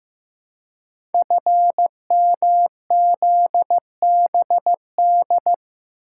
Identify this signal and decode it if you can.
Morse code, heard as a keyed tone, about 15 wpm, 700 Hz. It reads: FMZBD
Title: Code Groups - 5 Letters - 15wpm